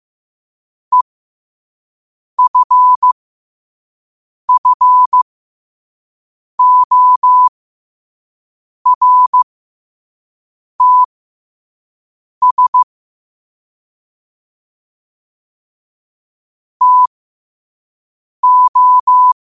Morse code, heard as a keyed tone, about 15 wpm, spaced as for 5 wpm, 1000 Hz. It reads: EFFORTS TO
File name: qotdfiles / 5wpm